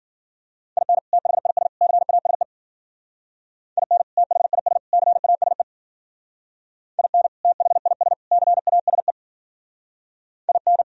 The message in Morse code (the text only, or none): IN THIS CASE IN THIS CASE IN THIS CASE IN